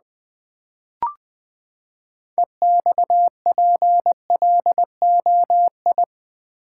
Morse code, heard as a keyed tone, about 20 wpm, 700 Hz. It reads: EXPLOI
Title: Build A Word - 6 Letters At A Time - 12-Letter Words - 20wpm